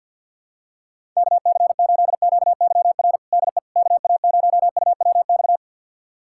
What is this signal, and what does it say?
Morse code, heard as a keyed tone, about 52 wpm, 700 Hz. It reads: KC8QYR DE KA0UWX
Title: USA Callsign DE Callsign - 6 Characters - 50wpm